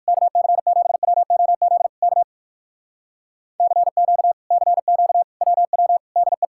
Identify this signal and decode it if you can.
Morse code, heard as a keyed tone, about 59 wpm, 700 Hz. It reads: KC7WOZ K CQ CQ WW DE